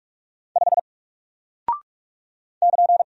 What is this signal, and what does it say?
Morse code, heard as a keyed tone, about 44 wpm, 700 Hz. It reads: 5 <KN>
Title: Single Character (Letter:Number:Punctuation:Prosign) 45wpm